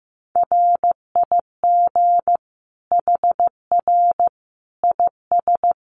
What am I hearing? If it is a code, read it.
Morse code, heard as a keyed tone, about 15 wpm, 700 Hz. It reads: RIG HR IS
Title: Common QSO Phrases 15wpm